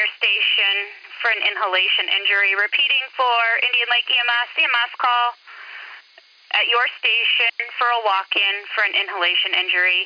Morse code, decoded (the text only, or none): none